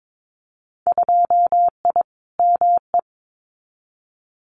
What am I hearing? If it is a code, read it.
Morse code, heard as a keyed tone, about 22 wpm, 700 Hz. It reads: 2I ME
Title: Winter Field Day Exchanges Encore 22wpm